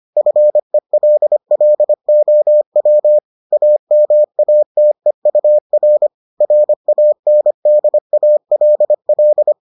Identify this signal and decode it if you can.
Morse code, heard as a keyed tone, about 25 wpm, 600 Hz. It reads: FELLOW AMATEUR RANDALL